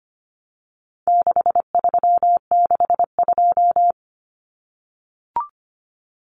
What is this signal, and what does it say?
Morse code, heard as a keyed tone, about 25 wpm, 700 Hz. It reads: 6362
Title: Numbers - 4-Digit 25wpm